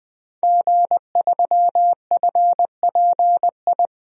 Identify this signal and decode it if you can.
Morse code, heard as a keyed tone, about 21 wpm, 700 Hz. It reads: G3FPI
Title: DXCC-100 Call Signs - 5 Characters - Repeated 2x - 20wpm